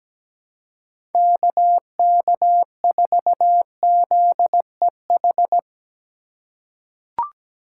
Morse code, heard as a keyed tone, about 17 wpm, 700 Hz. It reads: KK4ZEH
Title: USA Call Signs - 6 Characters - 2x Character Spacing - 17wpm